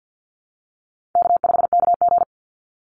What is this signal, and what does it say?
Morse code, heard as a keyed tone, about 50 wpm, 700 Hz. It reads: K5KG
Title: Easy Call Signs - 50wpm